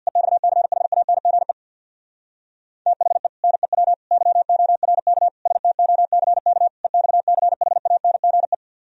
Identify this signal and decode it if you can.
Morse code, heard as a keyed tone, about 64 wpm, 700 Hz. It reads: EXCHANGE THE NEW YORK STOCK EXCHANGE